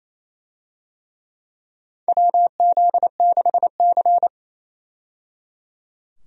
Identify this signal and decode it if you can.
Morse code, heard as a keyed tone, about 28 wpm, 700 Hz. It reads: WZ6C
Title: USA Call Signs - 4 Characters - 28wpm